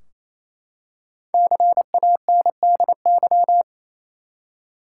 Morse code, heard as a keyed tone, about 28 wpm, 700 Hz. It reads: CANDY
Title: Five Letter Words - 28wpm